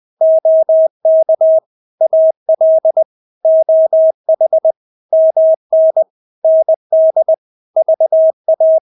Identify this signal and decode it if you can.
Morse code, heard as a keyed tone, about 20 wpm, 640 Hz. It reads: OK AL OH MN ND VA